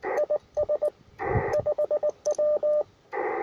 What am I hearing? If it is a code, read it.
Morse code, heard as a keyed tone, about 20 wpm, 600 Hz, with radio noise. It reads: IS 5W
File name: attachments / R7AO UR4MCK